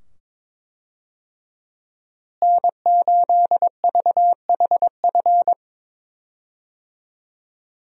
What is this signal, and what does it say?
Morse code, heard as a keyed tone, about 22 wpm, 700 Hz. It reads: N8VHF
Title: USA Call Signs - 5 Characters - Repeated 2x - 22wpm